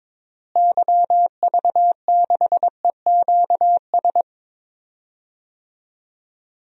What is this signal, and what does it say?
Morse code, heard as a keyed tone, about 22 wpm, 700 Hz. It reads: YV6EQS
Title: DXCC-100 Call Signs - 6 Characters - Repeated 3x - 22wpm